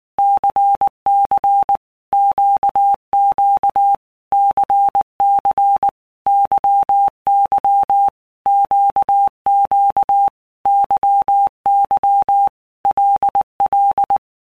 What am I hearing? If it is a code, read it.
Morse code, heard as a keyed tone, about 19 wpm, 800 Hz. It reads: CC QQ CC YY QQ YY LL
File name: Basic Letter Groups By Blocks / Groups@18 / LFCYQ@18